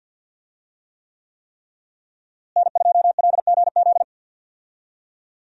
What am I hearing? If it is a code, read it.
Morse code, heard as a keyed tone, about 51 wpm, 700 Hz. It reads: N2LGZ
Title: USA Call Signs - 5 Characters - Repeated 3x - 50wpm